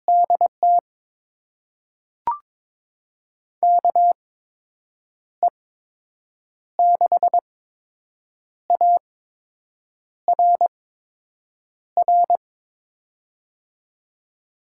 Morse code, heard as a keyed tone, about 22 wpm, spaced as for 7 wpm, 700 Hz. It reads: DT KE6ARR
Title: USA Call Signs - 6 Characters - 8x Character Spacing - 22wpm